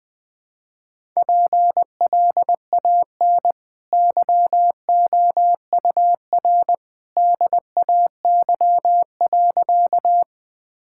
Morse code, heard as a keyed tone, about 21 wpm, 700 Hz. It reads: PLAN YOUR DAY.
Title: Easy Sentences with Hints - 20wpm